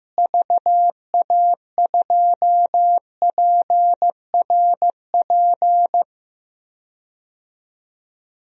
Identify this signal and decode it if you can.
Morse code, heard as a keyed tone, about 15 wpm, 700 Hz. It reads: VA2PRP
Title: DXCC-100 Call Signs - 6 Characters - Repeated 2x - 15wpm